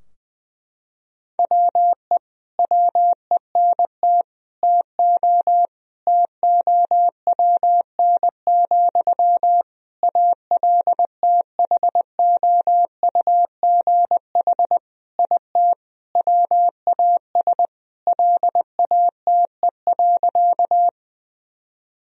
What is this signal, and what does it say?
Morse code, heard as a keyed tone, about 20 wpm, 700 Hz. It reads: WE WENT TO TOWN, ALTHOUGH IT WAS LATE.
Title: Sentences from Top 1000 Words - Review Part 3 - 20wpm